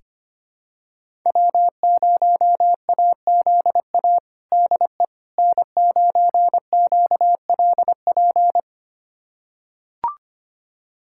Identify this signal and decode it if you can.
Morse code, heard as a keyed tone, about 25 wpm, 700 Hz. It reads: W0AZA DE N9QLP
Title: USA Callsign DE Callsign - 5 Characters - 25wpm